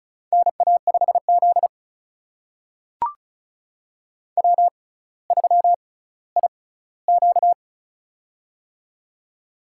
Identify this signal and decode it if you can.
Morse code, heard as a keyed tone, about 36 wpm, spaced as for 18 wpm, 700 Hz. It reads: NA5Z W3IQ
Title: USA Call Signs - 4 Characters - 6x Character Spacing - 35wpm